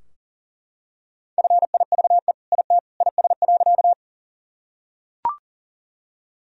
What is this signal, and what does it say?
Morse code, heard as a keyed tone, about 37 wpm, 700 Hz. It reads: FIVE IT IS.
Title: Sentences from Top 1000 Words - Review Part 2 - 40wpm